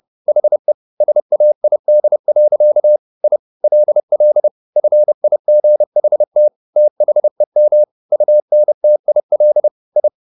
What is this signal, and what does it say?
Morse code, heard as a keyed tone, about 30 wpm, 600 Hz. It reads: HE SAID. I LL FIGHT THEM UNTIL I